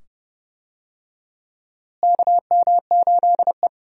Morse code, heard as a keyed tone, about 30 wpm, 700 Hz. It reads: KM8E
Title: USA Call Signs - 4 Characters - Repeated 2x - 30wpm